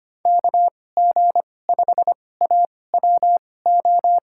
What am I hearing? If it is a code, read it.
Morse code, heard as a keyed tone, about 25 wpm, spaced as for 18 wpm, 700 Hz. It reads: KG5AWO
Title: USA Call Signs - 6 Characters - 2x Character Spacing - 25wpm